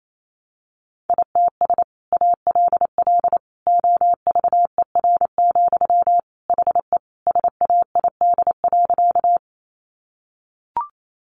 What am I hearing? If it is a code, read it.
Morse code, heard as a keyed tone, about 28 wpm, 700 Hz. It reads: ITS ALL OVER, HE SAID.
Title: Sentences from Top 600 Words 28wpm